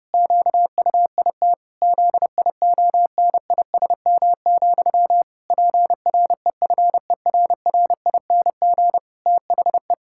Morse code, heard as a keyed tone, about 30 wpm, 700 Hz. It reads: QUIT ZIONISM, PREFERRING THE
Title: Pangram Challenge v2 - 30wpm